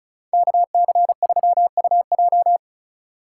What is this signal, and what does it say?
Morse code, heard as a keyed tone, about 38 wpm, 700 Hz. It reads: KC3UJ